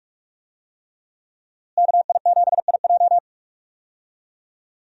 Morse code, heard as a keyed tone, about 43 wpm, 700 Hz. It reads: KI7IJ
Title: USA Call Signs - 5 Characters - Repeated 3x - 45wpm